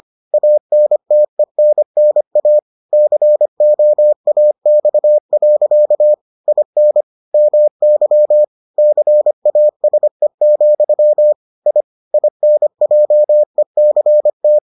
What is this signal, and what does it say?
Morse code, heard as a keyed tone, about 25 wpm, 600 Hz. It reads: ANTENNA COAX. IN MY CASE, I INJECT